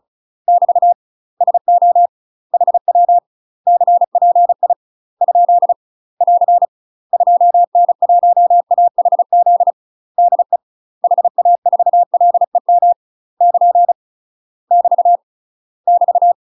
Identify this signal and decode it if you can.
Morse code, heard as a keyed tone, about 34 wpm, 700 Hz. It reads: <BT> SO HW CPI ? <AR> 2N1AHZ DE HA4LEM <KN> <BT> <BT>
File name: UK MASTER QSOs TRAINING / 35WPM / UK MASTER QSOs 0005